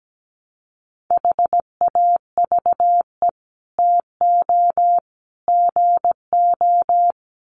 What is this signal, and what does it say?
Morse code, heard as a keyed tone, about 17 wpm, 700 Hz. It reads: HAVE TO GO
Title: Sets of 5 Words Encore 17wpm